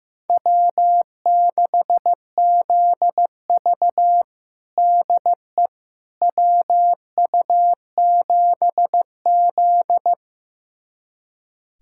Morse code, heard as a keyed tone, about 16 wpm, 700 Hz. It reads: W6ZV DE WU7Z
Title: USA Callsign DE Callsign - 4 Characters - 15wpm